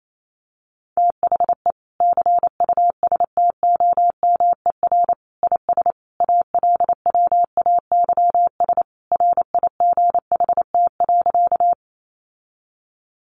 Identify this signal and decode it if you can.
Morse code, heard as a keyed tone, about 28 wpm, 700 Hz. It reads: THE CUSTOMER IS ALWAYS RIGHT.
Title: Sentences from Top 1100 Words 28wpm